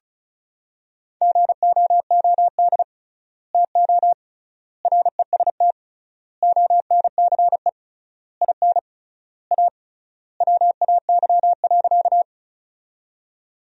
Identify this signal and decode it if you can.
Morse code, heard as a keyed tone, about 36 wpm, 700 Hz. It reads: GOOD TO REST ONCE IN A WAY.
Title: 3x Word-Spacing - Sentences from Top 500 Words - 35wpm